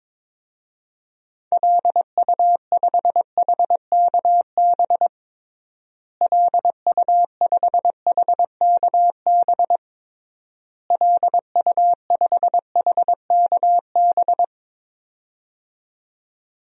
Morse code, heard as a keyed tone, about 22 wpm, 700 Hz. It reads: LU5HKB LU5HKB LU5HKB
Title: DXCC-100 Call Signs - 6 Characters - Repeated 3x - 22wpm